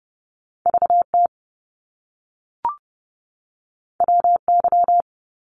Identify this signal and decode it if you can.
Morse code, heard as a keyed tone, about 30 wpm, 700 Hz. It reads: VT WY
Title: US States 30wpm